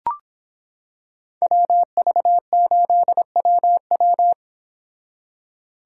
Morse code, heard as a keyed tone, about 26 wpm, 700 Hz. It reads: WV8WW